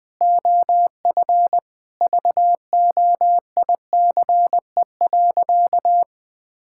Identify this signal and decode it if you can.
Morse code, heard as a keyed tone, about 20 wpm, 700 Hz. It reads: OF VOICE.
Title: Sentences from Top 1600 Words 20wpm